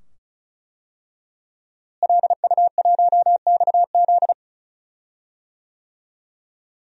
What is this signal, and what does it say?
Morse code, heard as a keyed tone, about 36 wpm, 700 Hz. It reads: LU1XZ
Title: DXCC-100 Call Signs - 5 Characters - Repeated 2x - 35wpm